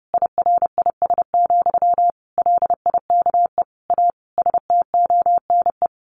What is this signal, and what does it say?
Morse code, heard as a keyed tone, about 28 wpm, 700 Hz. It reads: IRIS, LIKE A STONE
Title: A Sound Of Thunder 30wpm